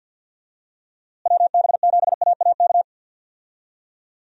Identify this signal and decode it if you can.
Morse code, heard as a keyed tone, about 50 wpm, 700 Hz. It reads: WB7AAK
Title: USA Call Signs - 6 Characters - Repeated 2x - 50wpm